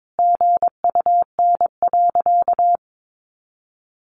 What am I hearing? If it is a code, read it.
Morse code, heard as a keyed tone, about 22 wpm, 700 Hz. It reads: GUN.
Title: A Sound Of Thunder 22wpm